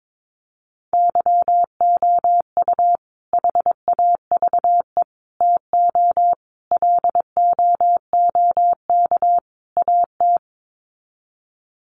Morse code, heard as a keyed tone, about 22 wpm, 700 Hz. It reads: YOU HAVE TO LOOK AT
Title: Sets of 5 Words Encore 22wpm